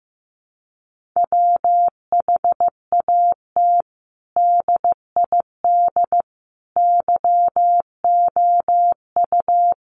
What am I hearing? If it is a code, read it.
Morse code, heard as a keyed tone, about 15 wpm, 700 Hz. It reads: WHAT DID YOU
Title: Sentences from Top 1400 Words 15wpm